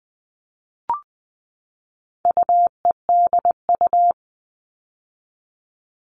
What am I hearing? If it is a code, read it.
Morse code, heard as a keyed tone, about 20 wpm, 700 Hz. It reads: UEDU
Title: All Cut Numbers - 4-Digit 20wpm